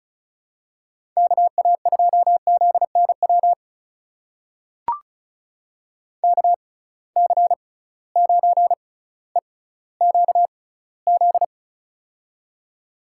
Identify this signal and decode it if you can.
Morse code, heard as a keyed tone, about 36 wpm, spaced as for 18 wpm, 700 Hz. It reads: KA2ZNW KC9EQZ
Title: USA Call Signs - 6 Characters - 6x Character Spacing - 35wpm